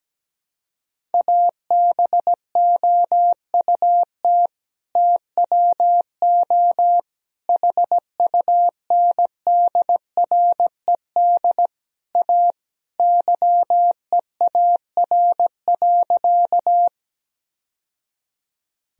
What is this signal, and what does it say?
Morse code, heard as a keyed tone, about 17 wpm, 700 Hz. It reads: ABOUT TWO HUNDRED A YEAR.
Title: Sentences from Top 1000 Words - Review Part 2 - 17wpm